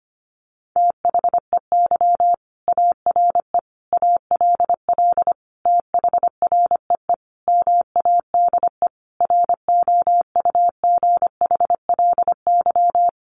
Morse code, heard as a keyed tone, about 25 wpm, 700 Hz. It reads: THEY ARE ALL THREE MADE ROUGHLY
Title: Sentences from Top 1500 Words 25wpm